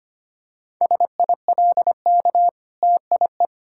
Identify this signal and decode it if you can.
Morse code, heard as a keyed tone, about 25 wpm, 700 Hz. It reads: SILK TIE
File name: practice / 25wpm / Difficult Words with Hints - 25wpm